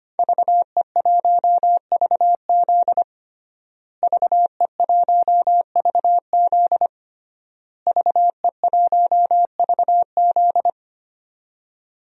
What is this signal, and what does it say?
Morse code, heard as a keyed tone, about 25 wpm, 700 Hz. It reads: VE1VZ VE1VZ VE1VZ
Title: DXCC-100 Call Signs - 5 Characters - Repeated 3x - 25wpm